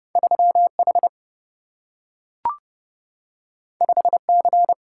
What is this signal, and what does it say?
Morse code, heard as a keyed tone, about 29 wpm, 700 Hz. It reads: 3H 5C
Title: Single-Letter Single-Number 30wpm